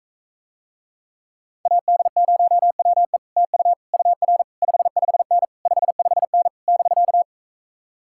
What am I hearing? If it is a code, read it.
Morse code, heard as a keyed tone, about 44 wpm, 700 Hz. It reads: AD0WE TU UR 55N 55N <BK>